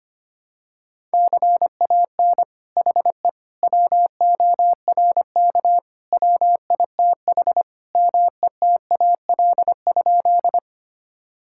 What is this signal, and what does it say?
Morse code, heard as a keyed tone, about 25 wpm, 700 Hz. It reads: CAN HE WORK WITH METAL?
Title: Sentences from Top 1000 Words - Review Part 6 - 25wpm